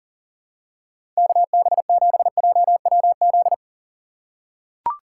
Morse code, heard as a keyed tone, about 40 wpm, 700 Hz. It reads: KB7JWZ
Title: USA Call Signs - 6 Characters - 3x Character Spacing - 40wpm